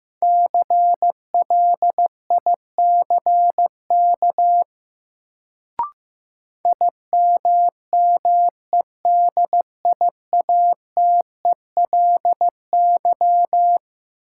Morse code, heard as a keyed tone, about 15 wpm, 700 Hz. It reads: CLICK IMMEDIATELY
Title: Sentences from Top 1100 Words - New Word Drill - 15wpm